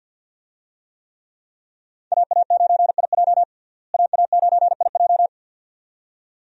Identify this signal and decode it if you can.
Morse code, heard as a keyed tone, about 50 wpm, 700 Hz. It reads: AA9IJ AA9IJ
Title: USA Call Signs - 5 Characters - Repeated 2x - 50wpm